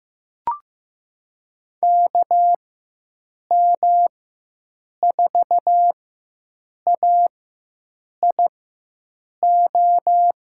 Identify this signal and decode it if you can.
Morse code, heard as a keyed tone, about 15 wpm, spaced as for 7 wpm, 700 Hz. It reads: KM4AIO
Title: USA Call Signs - 6 Characters - 4x Character Spacing - 15wpm